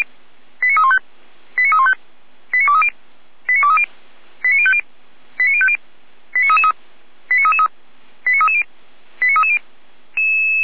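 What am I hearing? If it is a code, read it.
Five-tone selective call (ZVEI), heard as quick runs of five tones, tones of about 70 ms. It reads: 89315 89315 89230 89230 89050 89050 89333 89333 89330 89330
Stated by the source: ZVEI five-tone selective call 89304, received on SDR.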